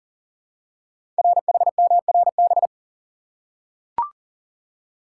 Morse code, heard as a keyed tone, about 41 wpm, 690 Hz. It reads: RHMRB
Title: Code Groups - 5 Letters - 40wpm